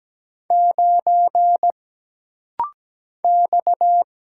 Morse code, heard as a keyed tone, about 17 wpm, 700 Hz. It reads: 9 X
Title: Single Letter-Number Alt - Rapid-Fire 17wpm